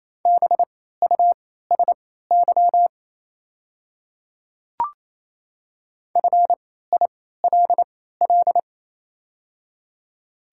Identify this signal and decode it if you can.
Morse code, heard as a keyed tone, about 28 wpm, spaced as for 16 wpm, 700 Hz. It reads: BUSY FILL
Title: 3x Character Spacing - Top 500 Words - 28wpm